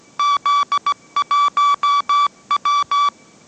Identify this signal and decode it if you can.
Morse code, heard as a keyed tone, about 17 wpm, 1150 Hz. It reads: Z1W
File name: mrx / mrx080 18wpm